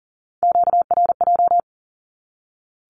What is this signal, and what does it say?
Morse code, heard as a keyed tone, about 40 wpm, 700 Hz. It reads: QRJ
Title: Q-Codes 40wpm